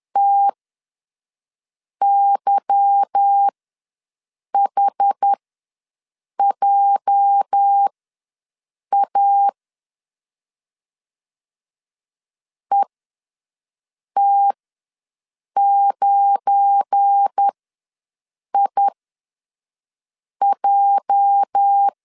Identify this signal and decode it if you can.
Morse code, heard as a keyed tone, about 10 wpm, spaced as for 5 wpm, 800 Hz. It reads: TYHJA ET9IJ
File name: LEZIONI / 8 YJ45 / 8 YJ ws 10wpm 4043